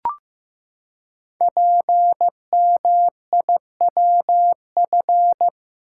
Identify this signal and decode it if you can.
Morse code, heard as a keyed tone, about 15 wpm, 700 Hz. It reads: PMIWF
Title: Code Groups - 5 Letter:Numbers - 15wpm